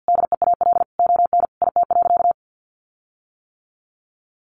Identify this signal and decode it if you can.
Morse code, heard as a keyed tone, about 48 wpm, 700 Hz. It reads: DEAL ON IT.
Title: Sentences from Top 1300 Words 50wpm